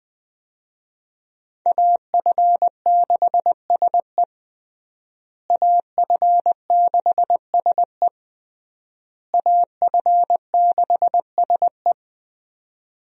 Morse code, heard as a keyed tone, about 20 wpm, 700 Hz. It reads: AF6SE AF6SE AF6SE
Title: USA Call Signs - 5 Characters - Repeated 3x - 20wpm